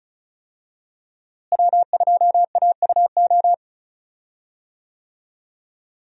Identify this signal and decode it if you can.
Morse code, heard as a keyed tone, about 36 wpm, 700 Hz. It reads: W2AUO
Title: USA Call Signs - 5 Characters - Repeated 3x - 35wpm